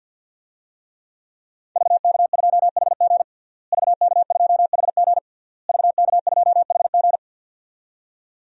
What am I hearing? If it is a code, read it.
Morse code, heard as a keyed tone, about 47 wpm, 700 Hz. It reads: VK2HG VK2HG VK2HG
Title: DXCC-100 Call Signs - 5 Characters - Repeated 3x - 50wpm